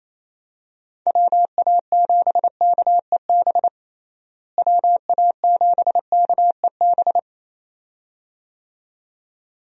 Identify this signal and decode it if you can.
Morse code, heard as a keyed tone, about 28 wpm, 700 Hz. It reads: WA7KEB WA7KEB
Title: USA Call Signs - 6 Characters - Repeated 2x - 28wpm